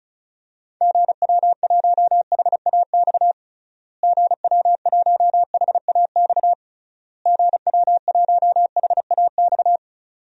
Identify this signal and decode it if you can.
Morse code, heard as a keyed tone, about 36 wpm, 700 Hz. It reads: GW1HAX GW1HAX GW1HAX
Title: DXCC-100 Call Signs - 6 Characters - Repeated 3x - 35wpm